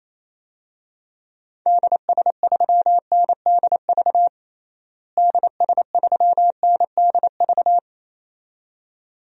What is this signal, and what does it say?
Morse code, heard as a keyed tone, about 28 wpm, 700 Hz. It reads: DS3NDV DS3NDV
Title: DXCC-100 Call Signs - 6 Characters - Repeated 2x - 28wpm